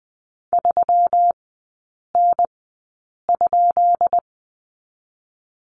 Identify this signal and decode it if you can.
Morse code, heard as a keyed tone, about 19 wpm, spaced as for 8 wpm, 700 Hz. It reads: 3N?
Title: 166 Lesson 27e - Number 7 - Three Characters 20wpm